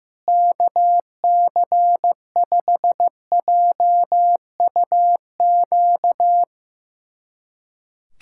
Morse code, heard as a keyed tone, about 15 wpm, 700 Hz. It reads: KC5JUQ
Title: USA Callsign DE Callsign - 6 Characters - 15wpm